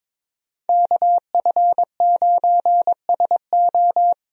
Morse code, heard as a keyed tone, about 22 wpm, 700 Hz. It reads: KF9SO
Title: USA Call Signs - 5 Characters - Repeated 3x - 22wpm